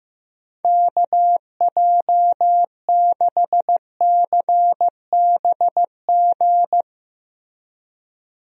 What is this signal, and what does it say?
Morse code, heard as a keyed tone, about 15 wpm, 700 Hz. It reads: KJ6CBG
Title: USA Call Signs - 6 Characters - 15wpm